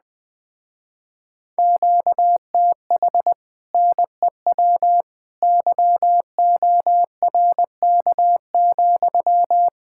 Morse code, heard as a keyed tone, about 20 wpm, 700 Hz. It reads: QTH NEW YORK,
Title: Speed-Racing Common QSO Phrases - 17wpm